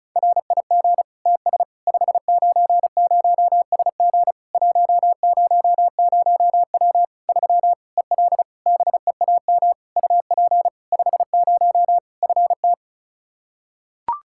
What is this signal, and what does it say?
Morse code, heard as a keyed tone, about 36 wpm, 700 Hz. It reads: RIG TS 590SG 100W 3 EL BEAM UP 50 FT
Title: K1USN SST QSO Practice 35wpm